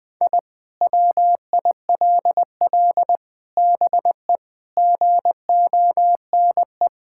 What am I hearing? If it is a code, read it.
Morse code, heard as a keyed tone, about 20 wpm, 700 Hz. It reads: I WILL BE GONE
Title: Sentences with 4-Word Phrases - 20wpm